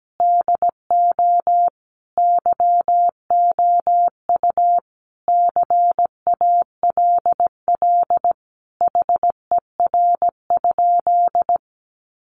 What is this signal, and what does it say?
Morse code, heard as a keyed tone, about 17 wpm, 700 Hz. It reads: DO YOU CALL HER?
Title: Sentences from Top 100 Words 17wpm